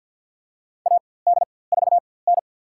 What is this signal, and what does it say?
Morse code, heard as a keyed tone, about 50 wpm, spaced as for 23 wpm, 700 Hz. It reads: AD4N
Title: USA Call Signs - 4x Character Spacing - 50wpm